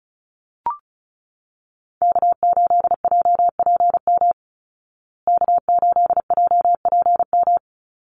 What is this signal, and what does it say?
Morse code, heard as a keyed tone, about 34 wpm, 700 Hz. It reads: K8JPM K8JPM
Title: Call Signs Repeated 2x - 35wpm